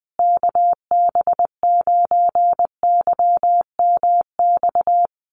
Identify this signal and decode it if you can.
Morse code, heard as a keyed tone, about 20 wpm, 700 Hz. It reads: KB9YMX
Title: USA Call Signs Repeated 3x 20wpm